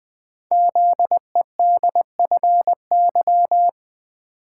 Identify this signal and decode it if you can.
Morse code, heard as a keyed tone, about 20 wpm, 700 Hz. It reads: ZEDFY
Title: Code Groups - 5 Letters - 20wpm